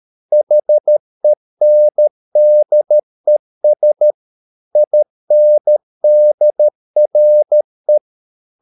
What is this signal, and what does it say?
Morse code, heard as a keyed tone, about 13 wpm, 600 Hz. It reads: HENDES INDRE
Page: Ferie 13 wpm